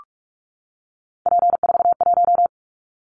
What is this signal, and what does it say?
Morse code, heard as a keyed tone, about 46 wpm, 700 Hz. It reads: P41
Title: Callsign Prefixes for Top 100 DXCC - 45wpm